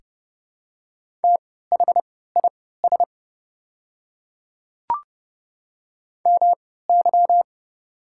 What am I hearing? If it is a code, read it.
Morse code, heard as a keyed tone, about 30 wpm, spaced as for 17 wpm, 700 Hz. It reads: THIS MY
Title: 3x Character Spacing - Top 200 Words - 30wpm